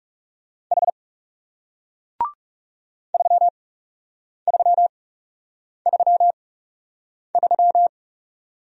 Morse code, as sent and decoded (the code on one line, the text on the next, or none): .... / ...-- ...-- ...-- ...--
H 3333